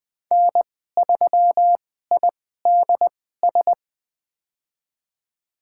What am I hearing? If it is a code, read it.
Morse code, heard as a keyed tone, about 20 wpm, spaced as for 14 wpm, 700 Hz. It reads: N3IDS
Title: USA Call Signs - 2x Character Spacing - 20wpm